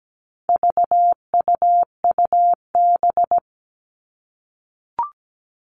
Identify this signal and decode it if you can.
Morse code, heard as a keyed tone, about 17 wpm, 700 Hz. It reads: VUUB
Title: All Cut Numbers - 4-Digit 17wpm